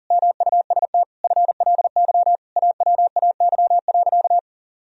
Morse code, heard as a keyed tone, about 40 wpm, 700 Hz. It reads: MUST FLY AWAY.
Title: Sentences from Top 1000 Words - Review Part 2 - 40wpm